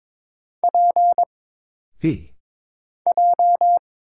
Morse code, heard as a keyed tone, about 22 wpm, 700 Hz. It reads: P J
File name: Single Character (Letter:Number:Punctuation:Prosign) - Mind-Melt - 22wpm